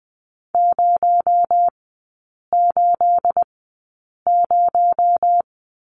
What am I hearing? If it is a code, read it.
Morse code, heard as a keyed tone, about 20 wpm, spaced as for 8 wpm, 700 Hz. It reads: 080
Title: 218 Lesson 35e - Number 0 - Three Characters 20wpm